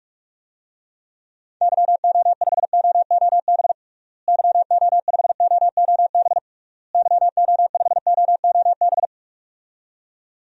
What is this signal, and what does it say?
Morse code, heard as a keyed tone, about 46 wpm, 700 Hz. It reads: YO5OOB YO5OOB YO5OOB
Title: DXCC-100 Call Signs - 6 Characters - Repeated 3x - 45wpm